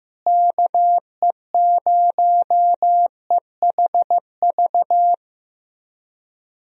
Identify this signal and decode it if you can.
Morse code, heard as a keyed tone, about 16 wpm, 700 Hz. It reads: KE0EHV
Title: USA Call Signs - 6 Characters - Repeated 2x - 15wpm